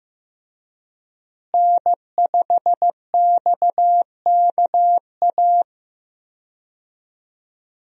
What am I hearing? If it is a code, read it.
Morse code, heard as a keyed tone, about 15 wpm, 700 Hz. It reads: N5XKA
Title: USA Call Signs - 5 Characters - Repeated 2x - 15wpm